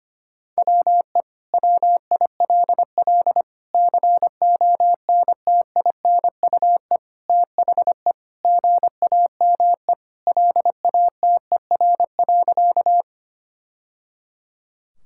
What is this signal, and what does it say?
Morse code, heard as a keyed tone, about 25 wpm, 700 Hz. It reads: WE WILL CONTINUE THE GAME LATER.
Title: Sentences from Top 1000 Words - Review Part 6 - 25wpm